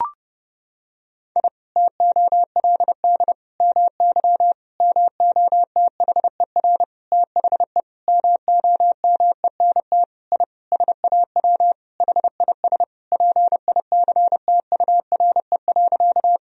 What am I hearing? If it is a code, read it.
Morse code, heard as a keyed tone, about 31 wpm, 700 Hz. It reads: I TOLD MY MOTHER THE MOMENT I SAW HIS PICTURE.
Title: Sentences from Top 1000 Words - Review Part 1 - 30wpm